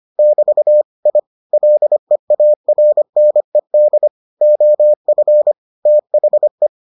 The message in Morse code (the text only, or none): <BT> I LEARNED OF THE